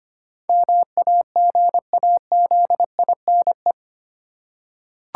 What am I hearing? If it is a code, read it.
Morse code, heard as a keyed tone, about 25 wpm, 700 Hz. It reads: MAGAZINE
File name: 8 Letter Words - First 3 Letters Given - 25wpm